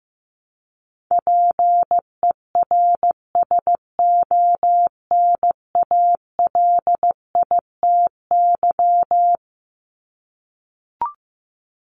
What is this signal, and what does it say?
Morse code, heard as a keyed tone, about 15 wpm, 700 Hz. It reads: PERSONALITY
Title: Top 1251-1500 Words 15wpm